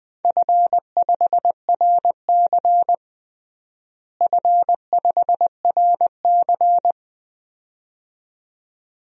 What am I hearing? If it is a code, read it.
Morse code, heard as a keyed tone, about 20 wpm, 700 Hz. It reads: F5RC F5RC
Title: DXCC-100 Call Signs - 4 Characters - Repeated 2x - 20wpm